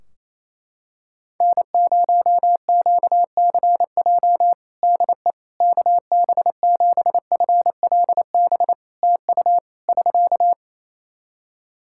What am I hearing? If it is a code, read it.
Morse code, heard as a keyed tone, about 28 wpm, 700 Hz. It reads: N0QCJ DE KB7FLB TU <SK>
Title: Practice QSO - Exchange 4 - Ending - 28wpm